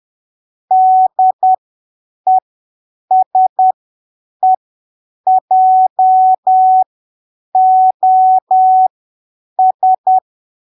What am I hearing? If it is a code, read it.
Morse code, heard as a keyed tone, about 10 wpm, spaced as for 7 wpm, 750 Hz. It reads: DESEJOS
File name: arquivos / texto / texto 23 10ppm